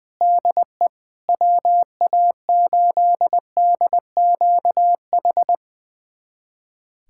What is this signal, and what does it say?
Morse code, heard as a keyed tone, about 21 wpm, 700 Hz. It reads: DE WA8DQH
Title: USA Callsign DE Callsign 20wpm